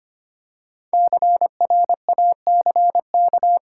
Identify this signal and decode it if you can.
Morse code, heard as a keyed tone, about 25 wpm, 700 Hz. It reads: CRACK
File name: 5 Letter Words - First 3 Letters Given - 25wpm